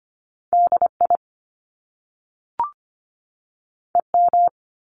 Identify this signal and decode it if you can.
Morse code, heard as a keyed tone, about 25 wpm, 700 Hz. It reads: DI EM
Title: Top Two Letter Combinations in Words 25wpm